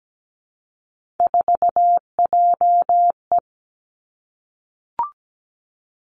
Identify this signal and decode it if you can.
Morse code, heard as a keyed tone, about 17 wpm, 700 Hz. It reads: 4JE
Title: Three Challenging Letter-Number Combinations 17wpm